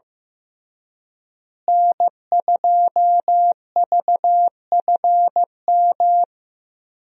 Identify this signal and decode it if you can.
Morse code, heard as a keyed tone, about 15 wpm, 700 Hz. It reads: N2VFM
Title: USA Call Signs - 5 Characters - Repeated 3x - 15wpm